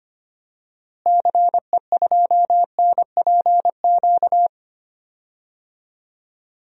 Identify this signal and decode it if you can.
Morse code, heard as a keyed tone, about 25 wpm, 700 Hz. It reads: CE2NPQ
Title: DXCC-100 Call Signs - 6 Characters - Repeated 3x - 25wpm